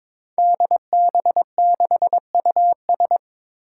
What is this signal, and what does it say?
Morse code, heard as a keyed tone, about 22 wpm, 700 Hz. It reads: DB6US
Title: Five Challenging Letter-Number Combinations 22wpm